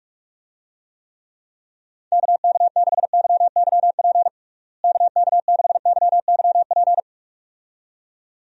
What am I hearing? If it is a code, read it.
Morse code, heard as a keyed tone, about 46 wpm, 700 Hz. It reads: KK6YYP KK6YYP
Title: USA Call Signs - 6 Characters - Repeated 2x - 45wpm